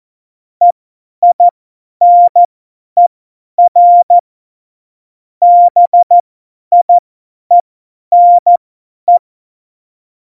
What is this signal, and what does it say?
Morse code, heard as a keyed tone, about 14 wpm, spaced as for 10 wpm, 700 Hz. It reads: EINER BIENE
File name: Morsekurs / Witze / Witz 06 March 2018, 50 F70